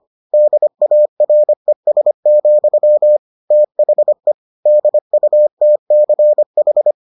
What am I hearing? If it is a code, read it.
Morse code, heard as a keyed tone, about 25 wpm, 600 Hz. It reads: DARES, THE DUTCH